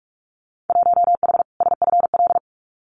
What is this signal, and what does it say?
Morse code, heard as a keyed tone, about 44 wpm, 700 Hz. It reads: 1H SFL
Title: Winter Field Day Exchanges 45wpm